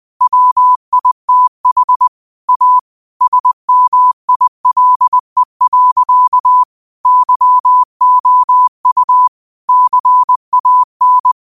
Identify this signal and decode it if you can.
Morse code, heard as a keyed tone, about 20 wpm, 1000 Hz. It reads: WITH A SMILE. YOU CAN